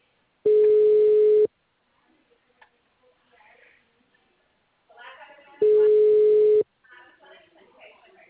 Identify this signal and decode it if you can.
Ringback tone (Europe, 425 Hz), with speech.
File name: out- -2019- - - .1227